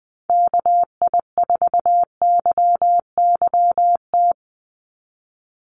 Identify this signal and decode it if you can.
Morse code, heard as a keyed tone, about 20 wpm, 700 Hz. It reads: KI4YYT
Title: USA Call Signs 20wpm